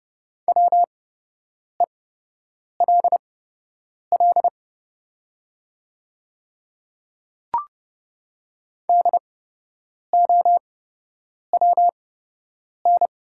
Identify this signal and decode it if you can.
Morse code, heard as a keyed tone, about 30 wpm, spaced as for 8 wpm, 690 Hz. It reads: WELL DOWN
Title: 8x Character Spacing - Top 200 Words - 30wpm